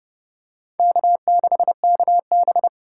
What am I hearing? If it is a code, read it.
Morse code, heard as a keyed tone, about 30 wpm, 700 Hz. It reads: K6KB
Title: USA Call Signs - 4 Characters - 3x Character Spacing - 30wpm